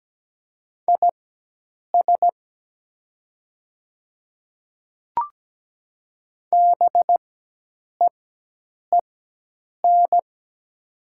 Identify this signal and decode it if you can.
Morse code, heard as a keyed tone, about 17 wpm, spaced as for 8 wpm, 700 Hz. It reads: IS BEEN
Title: 4x Character Spacing - Top 200 Words - 17wpm